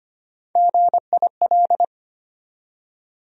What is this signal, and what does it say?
Morse code, heard as a keyed tone, about 25 wpm, 700 Hz. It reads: GIL
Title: USA Call Sign Postfixes - 3 Letters 25wpm